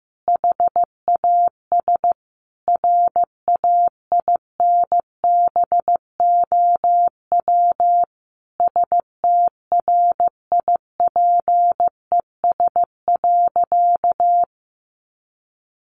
Morse code, heard as a keyed tone, about 14 wpm, 700 Hz. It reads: HAS RAINBOW STRIPES.
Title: Additional Easy Phrases 15wpm